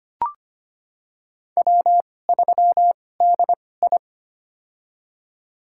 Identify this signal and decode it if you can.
Morse code, heard as a keyed tone, about 25 wpm, spaced as for 18 wpm, 700 Hz. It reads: W3DI
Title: USA Call Signs - 2x Character Spacing - 25wpm